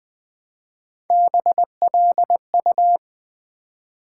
Morse code, heard as a keyed tone, about 20 wpm, 700 Hz. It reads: BLU
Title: Build A Word - 1 Letter At A Time - 4-Letter Words - 20wpm